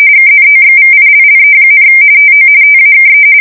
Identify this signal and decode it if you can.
RTTY, about 45.45 baud. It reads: KNOXVILLE, TO TAKE
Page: 170 Shift 2125/2295